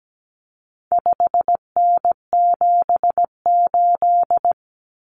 Morse code, heard as a keyed tone, about 17 wpm, 700 Hz. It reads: 5N78
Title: Common Cut Numbers - 4-Digit 17wpm